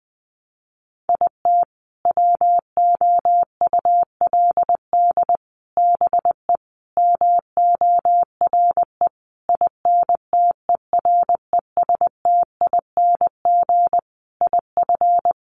Morse code, heard as a keyed tone, about 20 wpm, 700 Hz. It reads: IT WOULD BE MORE INTERESTING IF
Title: Sentences from Top 1000 Words 20wpm